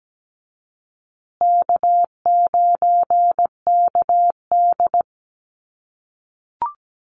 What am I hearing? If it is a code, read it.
Morse code, heard as a keyed tone, about 17 wpm, 700 Hz. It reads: K9KD
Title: USA Call Signs Repeated 3x 17wpm